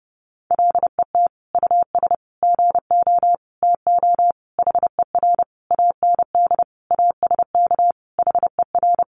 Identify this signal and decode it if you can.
Morse code, heard as a keyed tone, about 28 wpm, 700 Hz. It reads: LET US GO TO HER AND ASK HER
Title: Sentences from Top 1100 Words 30wpm